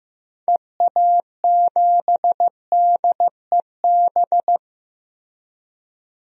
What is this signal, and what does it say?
Morse code, heard as a keyed tone, about 15 wpm, 700 Hz. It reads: EA7DEB
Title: DXCC-100 Call Signs - 6 Characters - Repeated 2x - 15wpm